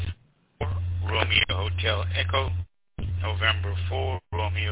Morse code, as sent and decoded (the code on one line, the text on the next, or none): none